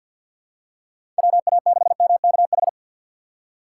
Morse code, heard as a keyed tone, about 52 wpm, 700 Hz. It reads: WA6MKH
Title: USA Call Signs - 6 Characters - 4x Character Spacing - 50wpm